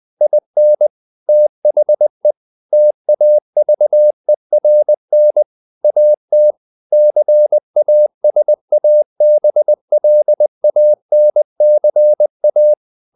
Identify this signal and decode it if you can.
Morse code, heard as a keyed tone, about 20 wpm, 600 Hz. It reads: IN THE TAVERN AT CASABLANCA
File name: the old man and the sea / the old man and the sea part 1 20wpm